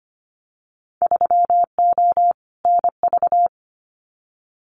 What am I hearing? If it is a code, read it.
Morse code, heard as a keyed tone, about 25 wpm, 700 Hz. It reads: 3O NV
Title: Winter Field Day Exchanges Encore 25wpm